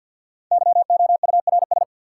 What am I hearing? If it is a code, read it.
Morse code, heard as a keyed tone, about 51 wpm, 700 Hz. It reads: YOURS
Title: Sentences from Top 1500 Words - New Word Drill Alt - 50wpm